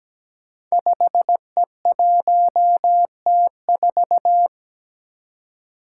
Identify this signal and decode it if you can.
Morse code, heard as a keyed tone, about 17 wpm, 700 Hz. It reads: 5E1T4
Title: Five Challenging Letter-Number Combinations 17wpm